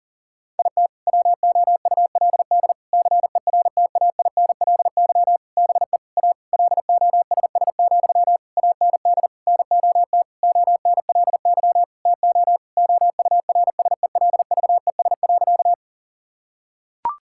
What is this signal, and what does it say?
Morse code, heard as a keyed tone, about 40 wpm, 700 Hz. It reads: IT WOULD CERTAINLY BE A LOSS, AND NOT ONLY TO OURSELVES.
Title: Sentences from Top 1600 Words 40wpm